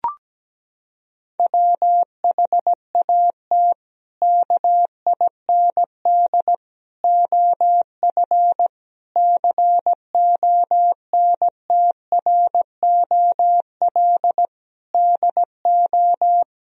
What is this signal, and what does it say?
Morse code, heard as a keyed tone, about 17 wpm, 700 Hz. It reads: WHAT KIND OF CONTROL DO
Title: Sentences from Top 1000 Words - Review Part 3 - 17wpm